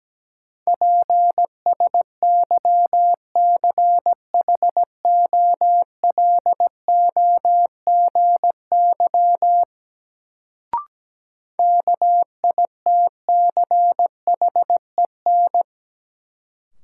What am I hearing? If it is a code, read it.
Morse code, heard as a keyed tone, about 17 wpm, 700 Hz. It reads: PSYCHOLOGY KITCHEN
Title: Sentences from Top 1400 Words - New Word Drill Alt - 17wpm